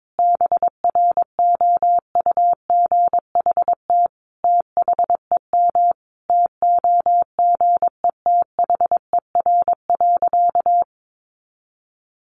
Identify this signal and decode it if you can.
Morse code, heard as a keyed tone, about 22 wpm, 700 Hz. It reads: BROUGHT THEM TOGETHER.
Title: Sentences from Top 1000 Words 22wpm